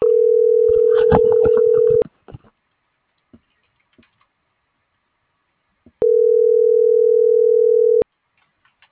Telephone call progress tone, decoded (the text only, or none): ringback tone